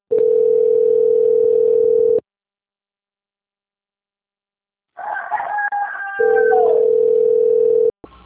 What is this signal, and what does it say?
Ringback tone (North America), 440 + 480 Hz.
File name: out- - - - - .41378